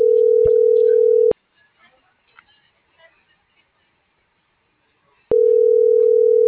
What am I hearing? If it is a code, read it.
Ringback tone (North America), 440 + 480 Hz.